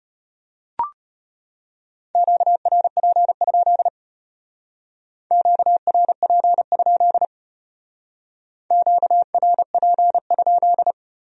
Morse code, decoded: QRP? QRP? QRP?

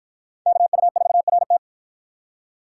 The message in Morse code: KU4RT

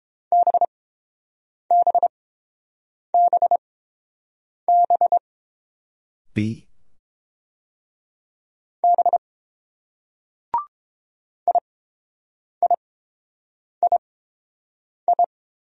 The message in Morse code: BBBBB IIII

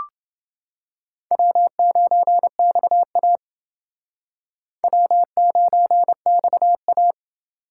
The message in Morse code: W9XA W9XA

